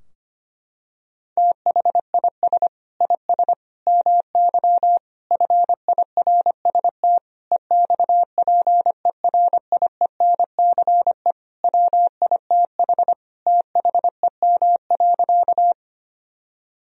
Morse code, heard as a keyed tone, about 25 wpm, 700 Hz. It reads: THIS IS MY FIRST EXPERIENCE WITH THEM.
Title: Sentences from Top 1000 Words - Review Part 3 - 25wpm